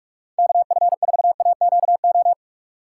Morse code, heard as a keyed tone, about 57 wpm, 700 Hz. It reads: KF4AQO